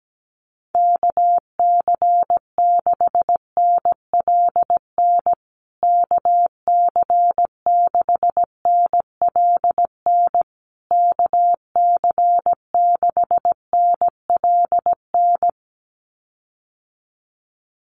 Morse code, heard as a keyed tone, about 17 wpm, 700 Hz. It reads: KC6NLN KC6NLN KC6NLN
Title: USA Call Signs Repeated 3x 17wpm